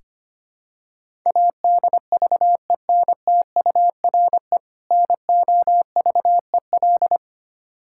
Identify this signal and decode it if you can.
Morse code, heard as a keyed tone, about 25 wpm, 700 Hz. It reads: ADVENTURE NOVEL
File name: practice / 25wpm / Difficult Words with Hints - 25wpm